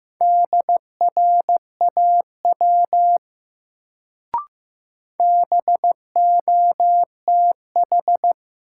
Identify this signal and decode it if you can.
Morse code, heard as a keyed tone, about 15 wpm, 700 Hz. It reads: DRAW BOTH
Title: Sentences from Top 300 Words - New Word Drill - 15wpm